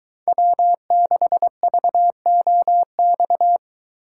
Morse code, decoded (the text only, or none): W6VOX